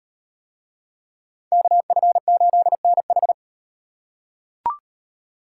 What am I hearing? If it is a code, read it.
Morse code, heard as a keyed tone, about 38 wpm, 700 Hz. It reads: KF8NH